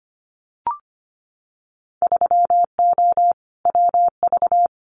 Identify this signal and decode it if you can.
Morse code, heard as a keyed tone, about 25 wpm, 700 Hz. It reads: 3O WV
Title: Winter Field Day Exchanges 25wpm